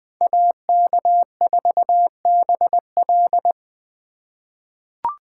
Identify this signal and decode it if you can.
Morse code, heard as a keyed tone, about 20 wpm, 700 Hz. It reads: AK4BL
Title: USA Call Signs - 5 Characters - 4x Character Spacing - 20wpm